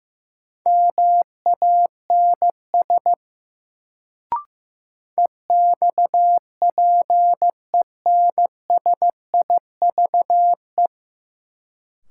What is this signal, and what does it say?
Morse code, heard as a keyed tone, about 15 wpm, 700 Hz. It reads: MANS EXPENSIVE
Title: Sentences from Top 1000 Words - Review Part 6 - New Word Drill - 15wpm